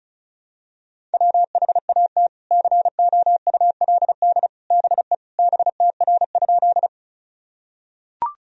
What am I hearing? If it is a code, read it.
Morse code, heard as a keyed tone, about 34 wpm, 700 Hz. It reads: WHAT COULD BE BTR?
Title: Sentences from Top 400 Words and Abbreviations 35wpm